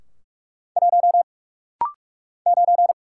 Morse code, heard as a keyed tone, about 44 wpm, 700 Hz. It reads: 1 9
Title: Single Letter-Number - Rapid-Fire 45wpm